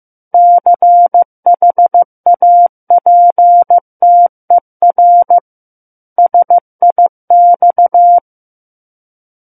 Morse code, heard as a keyed tone, about 15 wpm, 700 Hz. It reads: CHAPTER SIX
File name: Book One / 06-Chapter 6 - 15 WPM